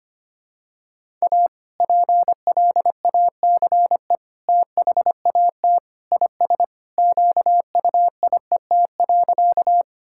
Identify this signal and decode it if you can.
Morse code, heard as a keyed tone, about 25 wpm, 700 Hz. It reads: A PLACE THAT IS QUIET.
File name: practice / 25wpm / Sentences with 2-Word Phrases - 25wpm